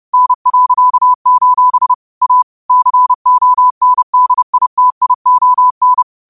Morse code, heard as a keyed tone, about 30 wpm, 1000 Hz. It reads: N.: A CONDITION